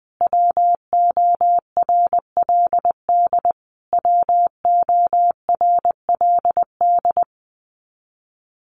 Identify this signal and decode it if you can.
Morse code, heard as a keyed tone, about 20 wpm, 700 Hz. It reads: WORLD WORLD
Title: Top 300 Words Repeated 20wpm